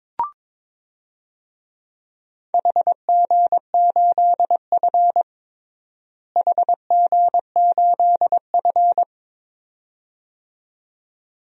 Morse code, heard as a keyed tone, about 22 wpm, 700 Hz. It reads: HG8F HG8F
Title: DXCC-100 Call Signs - 4 Characters - Repeated 2x - 22wpm